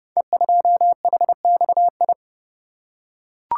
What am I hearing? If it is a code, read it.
Morse code, heard as a keyed tone, about 29 wpm, 700 Hz. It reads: E2HXI
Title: Code Groups - 5 Letter:Numbers - 30wpm